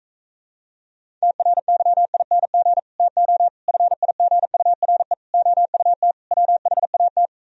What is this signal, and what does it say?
Morse code, heard as a keyed tone, about 45 wpm, 700 Hz. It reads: TRYING TO FIGURE OUT WHAT